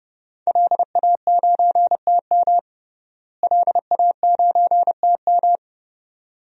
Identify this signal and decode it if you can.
Morse code, heard as a keyed tone, about 30 wpm, 700 Hz. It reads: LA9TM LA9TM
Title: DXCC-100 Call Signs - 5 Characters - Repeated 2x - 30wpm